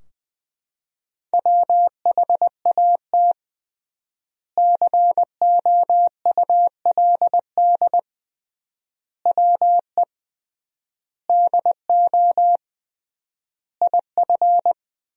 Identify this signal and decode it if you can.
Morse code, heard as a keyed tone, about 20 wpm, 700 Hz. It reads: WHAT COULD WE DO IF
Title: 3x Word-Spacing - Sentences from Top 400 Words - 20wpm